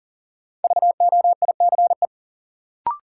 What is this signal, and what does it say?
Morse code, heard as a keyed tone, about 37 wpm, 700 Hz. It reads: VOICE
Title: Sentences from Top 500 Words - New Word Drill - 40wpm